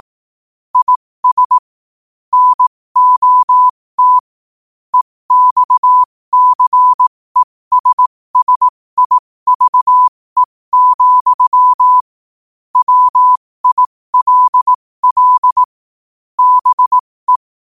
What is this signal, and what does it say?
Morse code, heard as a keyed tone, about 18 wpm, spaced as for 15 wpm, 1000 Hz. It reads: IS NOT EXCESSIVE, WILL BE
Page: New quotes every day in morse code at 15 Words per minute.